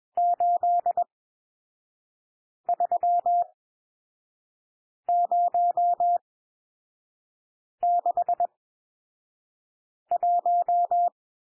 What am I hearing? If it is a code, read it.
Morse code, heard as a keyed tone, about 21 wpm, spaced as for 5 wpm, 700 Hz. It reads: 83061
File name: cwmp3 / N1 SOUND NUMBERS 21 wpm